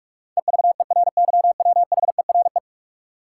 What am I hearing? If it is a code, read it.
Morse code, heard as a keyed tone, about 46 wpm, 700 Hz. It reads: EVERYWHERE
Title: Sentences from Top 1600 Words - New Word Drill - 45wpm